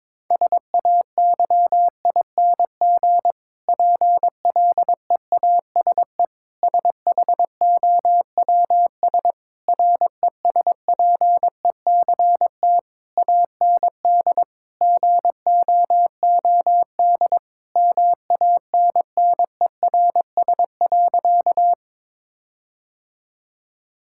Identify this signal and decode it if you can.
Morse code, heard as a keyed tone, about 22 wpm, 700 Hz. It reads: SAYING PLEASE SHOWS RESPECT AND GOOD MANNERS.
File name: Difficult Sentences with Hints - 22wpm